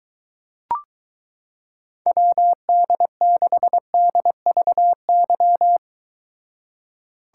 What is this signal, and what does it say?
Morse code, heard as a keyed tone, about 23 wpm, 700 Hz. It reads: WD6DVY